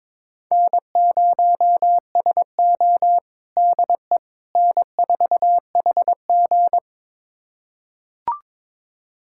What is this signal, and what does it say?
Morse code, heard as a keyed tone, about 21 wpm, 700 Hz. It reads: N0SO DE N4HG
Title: USA Callsign DE Callsign - 4 Characters - 22wpm